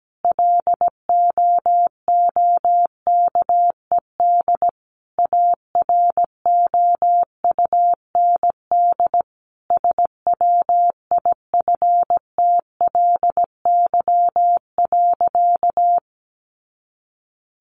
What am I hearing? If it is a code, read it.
Morse code, heard as a keyed tone, about 17 wpm, 700 Hz. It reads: LOOKED AROUND SWIFTLY.
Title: A Sound Of Thunder 17wpm